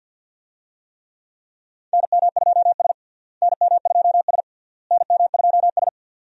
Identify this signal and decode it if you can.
Morse code, heard as a keyed tone, about 51 wpm, 700 Hz. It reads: NM2S NM2S NM2S
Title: USA Call Signs - 4 Characters - Repeated 3x - 50wpm